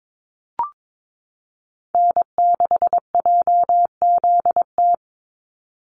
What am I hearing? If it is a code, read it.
Morse code, heard as a keyed tone, about 22 wpm, 700 Hz. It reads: N6JZT
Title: NA SOTA Activator Callsigns 22wpm